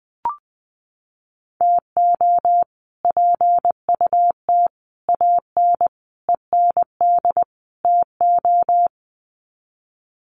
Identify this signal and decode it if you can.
Morse code, heard as a keyed tone, about 20 wpm, 700 Hz. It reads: TO PUT AN END TO
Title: Sets of 5 Words Encore2 20wpm